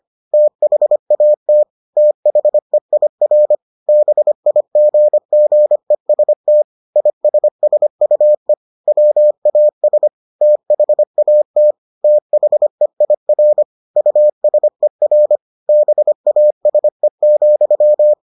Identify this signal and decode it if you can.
Morse code, heard as a keyed tone, about 25 wpm, 600 Hz. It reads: THAT THEIR BIGGEST ISSUE WAS THAT THEIR USER BASE,